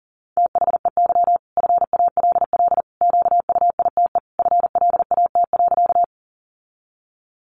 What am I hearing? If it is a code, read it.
Morse code, heard as a keyed tone, about 40 wpm, 700 Hz. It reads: THEY FALL QUITE FLAT.
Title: Sentences from Top 1000 Words 40wpm